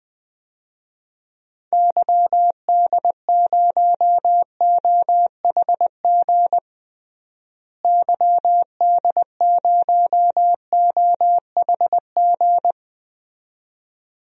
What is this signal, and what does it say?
Morse code, heard as a keyed tone, about 20 wpm, 700 Hz. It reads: YD0OHG YD0OHG
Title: DXCC-100 Call Signs - 6 Characters - Repeated 2x - 20wpm